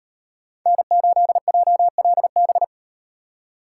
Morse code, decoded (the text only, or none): N8JLB